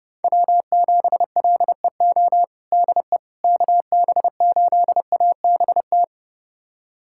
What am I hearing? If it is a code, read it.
Morse code, heard as a keyed tone, about 29 wpm, 700 Hz. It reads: W7LEO DE KB8ABT
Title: USA Callsign DE Callsign 30wpm